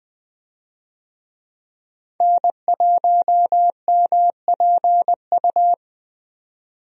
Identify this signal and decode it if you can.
Morse code, heard as a keyed tone, about 20 wpm, 700 Hz. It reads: N1MPU
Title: USA Call Signs - 5 Characters - 20wpm